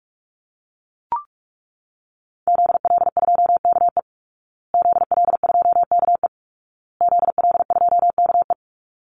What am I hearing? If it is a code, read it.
Morse code, heard as a keyed tone, about 44 wpm, 700 Hz. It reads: ZL2KE ZL2KE ZL2KE
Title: Call Signs Repeated 2x - 45wpm